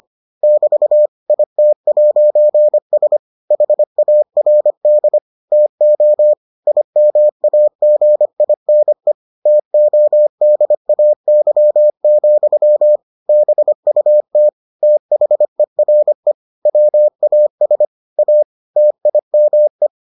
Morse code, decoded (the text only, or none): <BT> IT'S HARD TO IMAGINE TODAY, BUT THERE WAS A TIME